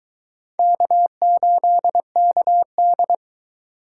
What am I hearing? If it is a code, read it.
Morse code, heard as a keyed tone, about 22 wpm, 700 Hz. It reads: K8KD